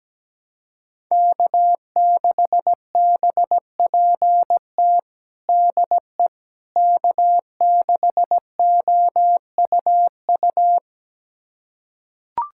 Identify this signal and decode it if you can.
Morse code, heard as a keyed tone, about 17 wpm, 700 Hz. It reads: K6BPT DE K6OUU
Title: USA Callsign DE Callsign - 5 Characters - 17wpm